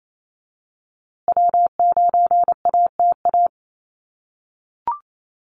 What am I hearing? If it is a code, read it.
Morse code, heard as a keyed tone, about 28 wpm, 700 Hz. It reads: W9ATA
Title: USA Call Signs Repeated 3x 28wpm